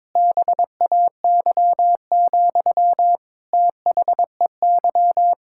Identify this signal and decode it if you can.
Morse code, heard as a keyed tone, about 22 wpm, 700 Hz. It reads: BAY, THEY
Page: morse-code-ninja-practice-22wpm